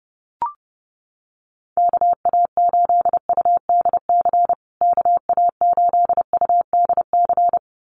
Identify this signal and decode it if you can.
Morse code, heard as a keyed tone, about 30 wpm, 700 Hz. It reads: KA8UDC KA8UDC
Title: USA Call Signs Repeated 2x 30wpm